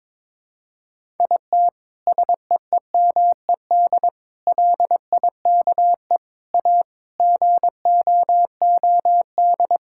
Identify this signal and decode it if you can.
Morse code, heard as a keyed tone, about 22 wpm, 700 Hz. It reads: IT SEEMED LIKE A GOOD
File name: Sets of 9 Words - 22wpm